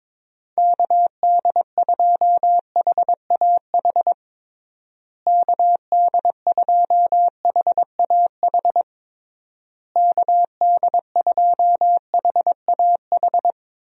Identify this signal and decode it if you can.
Morse code, heard as a keyed tone, about 22 wpm, 700 Hz. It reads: KD2HAH KD2HAH KD2HAH
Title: USA Call Signs - 6 Characters - Repeated 3x - 22wpm